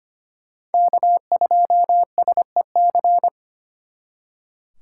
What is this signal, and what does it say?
Morse code, heard as a keyed tone, about 31 wpm, 700 Hz. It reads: K2SEC
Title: Speed Racing - USA Call Signs 25wpm